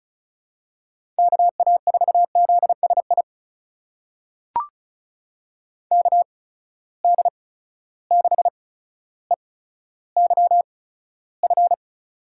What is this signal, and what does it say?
Morse code, heard as a keyed tone, about 35 wpm, spaced as for 15 wpm, 700 Hz. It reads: KA4ZSI KD6EYF
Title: USA Call Signs - 6 Characters - 8x Character Spacing - 35wpm